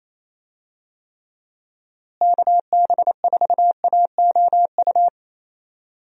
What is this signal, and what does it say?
Morse code, heard as a keyed tone, about 28 wpm, 700 Hz. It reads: KB4AOU
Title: USA Call Signs - 6 Characters - Repeated 3x - 28wpm